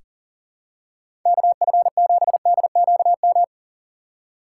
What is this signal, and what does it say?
Morse code, heard as a keyed tone, about 37 wpm, 700 Hz. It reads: KF7DQM
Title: USA Call Signs - 6x Character Spacing - 40wpm